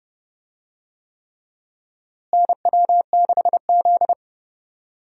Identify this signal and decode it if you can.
Morse code, heard as a keyed tone, about 30 wpm, 700 Hz. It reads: NW6Z
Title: USA Call Signs - 4 Characters - Repeated 3x - 30wpm